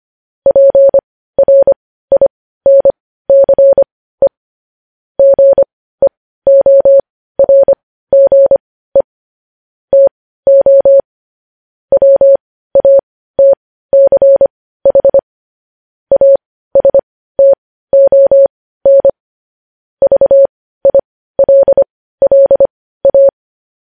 Morse code, heard as a keyed tone, about 25 wpm, spaced as for 15 wpm, 550 Hz. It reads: PRINCE GEORGE TO WATCH ASTON VILLA
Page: news-headlines-in-morse-code-at-15-wpm